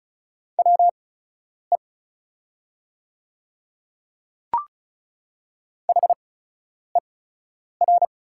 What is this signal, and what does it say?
Morse code, heard as a keyed tone, about 36 wpm, spaced as for 10 wpm, 690 Hz. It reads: WE HER
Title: 8x Character Spacing - Top 100 Words - 35wpm